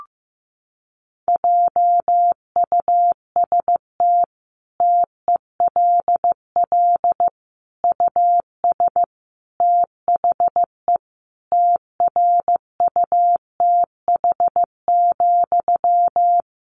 Morse code, heard as a keyed tone, about 15 wpm, 700 Hz. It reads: JUST TELL US THE TRUTH,
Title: Sentences from Top 800 Words 15wpm